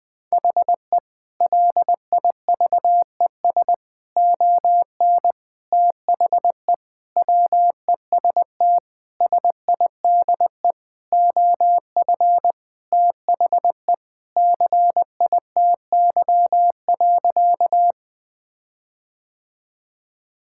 Morse code, heard as a keyed tone, about 21 wpm, 700 Hz. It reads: HE LIVES ON THE WEST SIDE OF THE CITY.
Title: Sentences with 6-Word Phrases - 20wpm